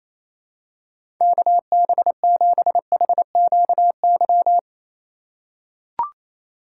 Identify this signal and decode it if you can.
Morse code, heard as a keyed tone, about 28 wpm, 700 Hz. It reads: KB7HQY
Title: USA Call Signs - 2x Character Spacing - 28wpm